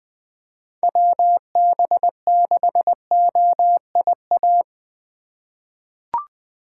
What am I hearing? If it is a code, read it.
Morse code, heard as a keyed tone, about 20 wpm, 700 Hz. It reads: WB6OIA
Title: USA Call Signs - 6 Characters - 3x Character Spacing - 20wpm